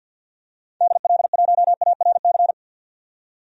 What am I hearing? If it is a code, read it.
Morse code, heard as a keyed tone, about 52 wpm, 700 Hz. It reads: DL1ARC
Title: DXCC-100 Call Signs - 6 Characters - 50wpm